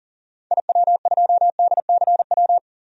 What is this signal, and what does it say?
Morse code, heard as a keyed tone, about 40 wpm, 700 Hz. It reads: IW2DCW
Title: DXCC-100 Call Signs - 6 Characters - 40wpm